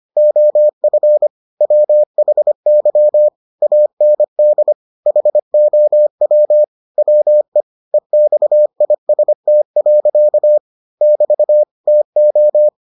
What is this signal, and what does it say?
Morse code, heard as a keyed tone, about 25 wpm, 600 Hz. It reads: OF WHY AND HOW WE EXIST. <BT> TO